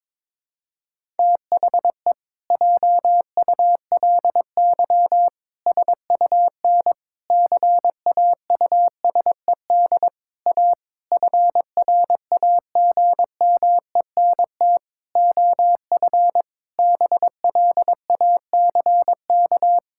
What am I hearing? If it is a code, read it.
Morse code, heard as a keyed tone, about 22 wpm, 700 Hz. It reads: THE JULY SUN CAUSED A FRAGMENT OF BLACK
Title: Pangram Challenge v2 - 22wpm